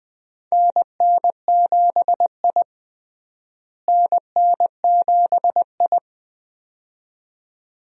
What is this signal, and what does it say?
Morse code, heard as a keyed tone, about 20 wpm, 700 Hz. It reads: NN7I NN7I
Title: USA Call Signs - 4 Characters - Repeated 2x - 20wpm